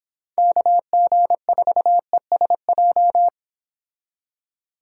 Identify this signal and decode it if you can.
Morse code, heard as a keyed tone, about 26 wpm, 700 Hz. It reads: KG4ESJ
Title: Speed Racing - USA Call Signs 22wpm